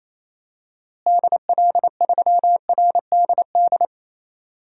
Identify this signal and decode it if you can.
Morse code, heard as a keyed tone, about 27 wpm, 700 Hz. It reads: DL3RDD
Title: DXCC-100 Call Signs - 6 Characters - Repeated 3x - 28wpm